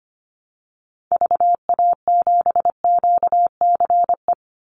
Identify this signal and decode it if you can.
Morse code, heard as a keyed tone, about 25 wpm, 700 Hz. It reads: VA7QCE
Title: Call Signs Repeated 3x - 25wpm